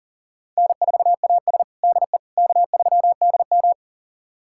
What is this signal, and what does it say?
Morse code, heard as a keyed tone, about 40 wpm, 700 Hz. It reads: N4AS DE K3DM
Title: USA Callsign DE Callsign - 4 Characters - 40wpm